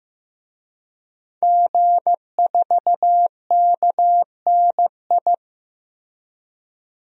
Morse code, heard as a keyed tone, about 15 wpm, 700 Hz. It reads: G4KNI
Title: DXCC-100 Call Signs - 5 Characters - 15wpm